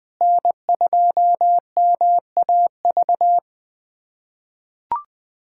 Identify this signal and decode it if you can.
Morse code, heard as a keyed tone, about 21 wpm, 700 Hz. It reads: N2MAV
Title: USA Callsign DE Callsign - 5 Characters - 20wpm